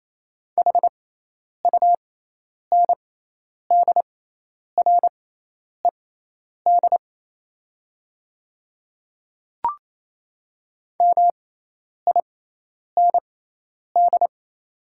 Morse code, heard as a keyed tone, about 27 wpm, spaced as for 10 wpm, 690 Hz. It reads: HUNDRED MIND
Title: 6x Character Spacing - Top 500 Words - 28wpm